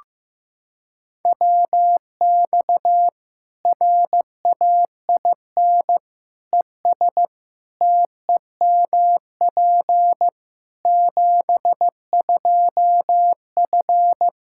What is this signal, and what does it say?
Morse code, heard as a keyed tone, about 15 wpm, 700 Hz. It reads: WX RAIN ES TEMP 72F
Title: K1USN SST QSO Practice 15wpm